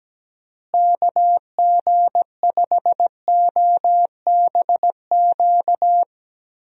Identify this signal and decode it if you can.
Morse code, heard as a keyed tone, about 17 wpm, 700 Hz. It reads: KG5OBQ
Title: USA Call Signs - 6 Characters - Repeated 2x - 17wpm